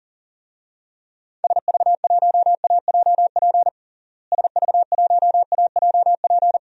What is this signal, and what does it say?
Morse code, heard as a keyed tone, about 40 wpm, 700 Hz. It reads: SV1AJP SV1AJP
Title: DXCC-100 Call Signs - 6 Characters - Repeated 3x - 40wpm